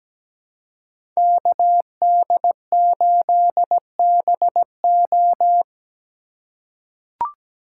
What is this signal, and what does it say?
Morse code, heard as a keyed tone, about 17 wpm, 700 Hz. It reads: KD8BO
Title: USA Call Signs - 4x Character Spacing - 17wpm